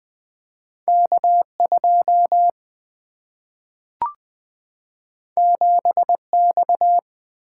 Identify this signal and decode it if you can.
Morse code, heard as a keyed tone, about 20 wpm, 700 Hz. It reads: K2 7X
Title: Single-Letter Single-Number 20wpm